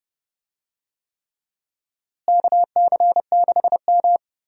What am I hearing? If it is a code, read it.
Morse code, heard as a keyed tone, about 30 wpm, 700 Hz. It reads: KC6M
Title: USA Call Signs - 4 Characters - Repeated 3x - 30wpm